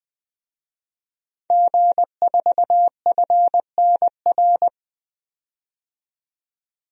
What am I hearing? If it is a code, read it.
Morse code, heard as a keyed tone, about 20 wpm, 700 Hz. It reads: G4FNR
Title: DXCC-100 Call Signs - 5 Characters - 20wpm